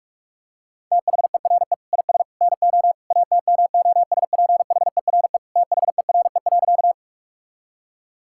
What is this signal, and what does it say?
Morse code, heard as a keyed tone, about 46 wpm, 700 Hz. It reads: THERE IS NO ATMOSPHERE THERE.
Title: Sentences from Top 1600 Words 45wpm